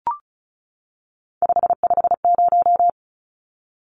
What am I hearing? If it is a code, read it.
Morse code, heard as a keyed tone, about 34 wpm, 700 Hz. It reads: 550
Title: Numbers - 3-Digit 35wpm